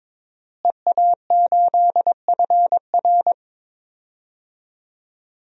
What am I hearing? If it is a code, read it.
Morse code, heard as a keyed tone, about 22 wpm, 700 Hz. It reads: EA8FR
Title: DXCC-100 Call Signs - 5 Characters - Repeated 3x - 22wpm